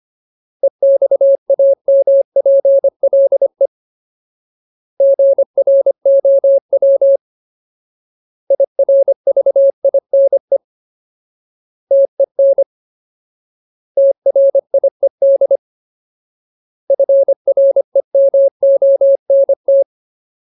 Morse code, heard as a keyed tone, about 25 wpm, 550 Hz. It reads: EXAMPLE GROW IRVINE TEN TRIED FREMONT
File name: Ultimate Word List / 25WPM Ultimate